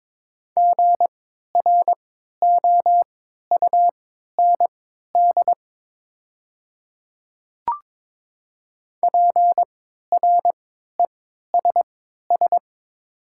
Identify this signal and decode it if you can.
Morse code, heard as a keyed tone, about 22 wpm, spaced as for 12 wpm, 700 Hz. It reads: GROUND PRESS
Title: 3x Character Spacing - Top 500 Words - 22wpm